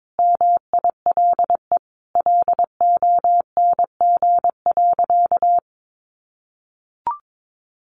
Morse code, heard as a keyed tone, about 22 wpm, 700 Hz. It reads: MILE LONG.
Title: Sentences from Top 1300 Words 22wpm